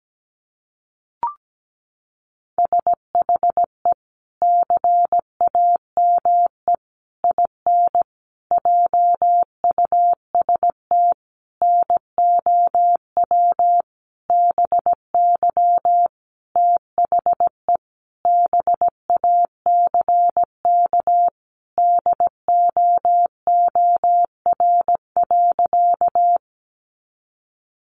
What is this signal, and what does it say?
Morse code, heard as a keyed tone, about 17 wpm, 700 Hz. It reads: SHE CAME IN JUST NOW BY THE BACK DOOR.
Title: Sentences from Top 300 Words 17wpm